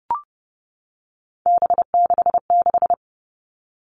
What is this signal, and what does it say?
Morse code, heard as a keyed tone, about 30 wpm, 700 Hz. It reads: B66
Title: All Cut Numbers - 3-Digit 30wpm